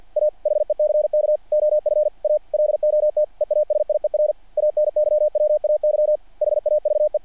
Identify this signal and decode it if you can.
Morse code, heard as a keyed tone, about 48 wpm, 660 Hz. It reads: A HECK OF A LOT EASIER ANYWAY HAVE
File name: archived / CRSnet